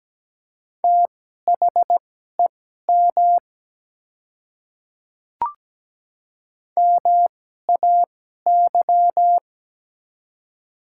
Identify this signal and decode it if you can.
Morse code, heard as a keyed tone, about 17 wpm, spaced as for 12 wpm, 700 Hz. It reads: THEM MAY
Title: 2x Character Spacing - Top 100 Words - 17wpm